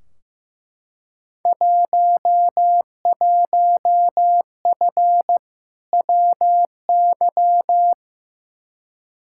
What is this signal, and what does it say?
Morse code, heard as a keyed tone, about 15 wpm, 700 Hz. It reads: 11F WY
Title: Field Day Exchanges Encore - 15wpm